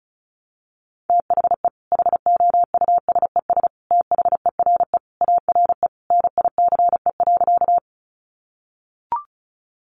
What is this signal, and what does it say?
Morse code, heard as a keyed tone, about 34 wpm, 700 Hz. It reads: THE HOUSES THERE ARE NICE.
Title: Easy Phrases 35wpm